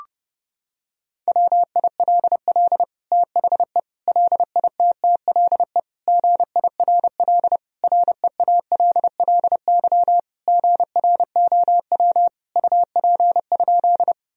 Morse code, heard as a keyed tone, about 32 wpm, 700 Hz. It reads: WILL THE LITTLE GIRL REALLY GROW UP?
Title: Sentences from Top 1000 Words - Review Part 1 - 30wpm